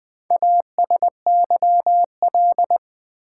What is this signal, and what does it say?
Morse code, heard as a keyed tone, about 20 wpm, 700 Hz. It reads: ASYL
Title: Build A Word - 2 Letters At A Time - 6-Letter Words - 20wpm